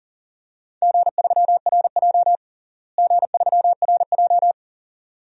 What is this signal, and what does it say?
Morse code, heard as a keyed tone, about 37 wpm, 700 Hz. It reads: G3RJ G3RJ
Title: DXCC-100 Call Signs - 4 Characters - Repeated 2x - 40wpm